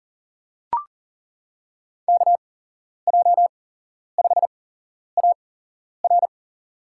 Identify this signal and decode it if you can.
Morse code, heard as a keyed tone, about 37 wpm, spaced as for 11 wpm, 700 Hz. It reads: KJ5AR
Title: USA Call Signs - 8x Character Spacing - 40wpm